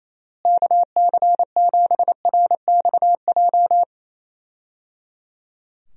Morse code, heard as a keyed tone, about 28 wpm, 700 Hz. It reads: KC7RXJ
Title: USA Call Signs - 6 Characters - Repeated 3x - 28wpm